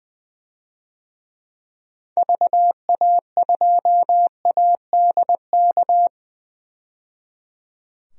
Morse code, heard as a keyed tone, about 21 wpm, 700 Hz. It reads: VA2ADK
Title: DXCC-100 Call Signs - 6 Characters - 20wpm